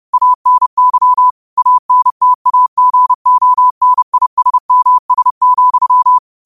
Morse code, heard as a keyed tone, about 30 wpm, 1000 Hz. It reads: ANY ANTAGONISMS,